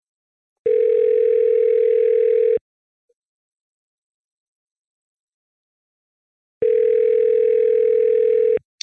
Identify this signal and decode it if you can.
Ringback tone (North America), 440 + 480 Hz.